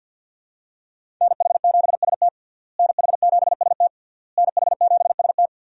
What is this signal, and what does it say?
Morse code, heard as a keyed tone, about 52 wpm, 700 Hz. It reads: NH7ST NH7ST NH7ST
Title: USA Call Signs - 5 Characters - Repeated 3x - 50wpm